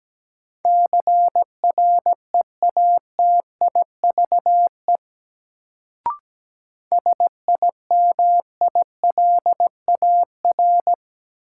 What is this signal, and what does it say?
Morse code, heard as a keyed tone, about 17 wpm, 700 Hz. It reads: CREATIVE SIMILAR
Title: Sentences from Top 1000 Words - Review Part 5 - New Word Drill - 17wpm